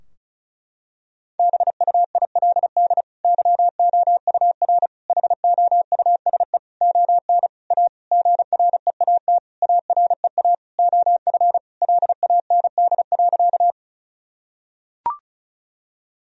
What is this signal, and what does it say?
Morse code, heard as a keyed tone, about 34 wpm, 700 Hz. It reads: BUILD YOUR HOUSE ON A GREAT AREA OF LAND.
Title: Sentences from Top 1000 Words - Review Part 2 - 35wpm